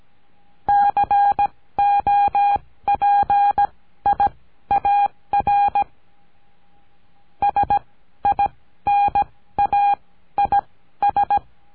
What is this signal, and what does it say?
Morse code, heard as a keyed tone, about 17 wpm, spaced as for 13 wpm, 810 Hz. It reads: COPIAR SINAIS
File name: CURSO DE TELEGRAFIA